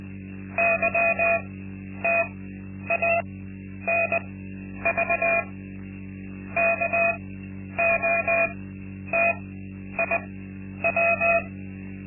Morse code, heard as a keyed tone, about 20 wpm, spaced as for 10 wpm, 680 Hz. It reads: YTANV KOTIW
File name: Linksmakalnis-2